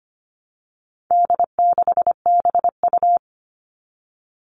Morse code, heard as a keyed tone, about 25 wpm, 700 Hz. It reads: D6BU
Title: All Cut Numbers - 4-Digit 25wpm